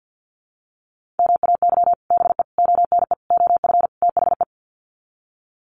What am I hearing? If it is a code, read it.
Morse code, heard as a keyed tone, about 50 wpm, 700 Hz. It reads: MAY BE ONE OF THE
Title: Sets of 5 Words Encore2 50wpm